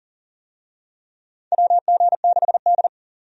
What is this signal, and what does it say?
Morse code, heard as a keyed tone, about 37 wpm, 700 Hz. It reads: WG6D
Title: USA Call Signs - 4 Characters - 40wpm